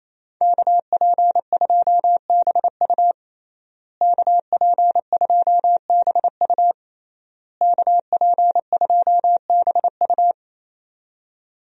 Morse code, heard as a keyed tone, about 28 wpm, 700 Hz. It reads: KP2BU KP2BU KP2BU
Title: USA Call Signs - 5 Characters - Repeated 3x - 28wpm